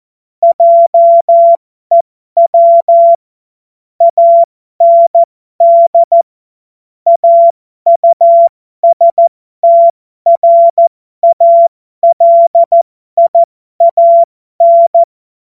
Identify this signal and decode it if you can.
Morse code, heard as a keyed tone, about 14 wpm, spaced as for 12 wpm, 680 Hz. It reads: JEW AND AUSTRALIAN
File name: Morsekurs / Witze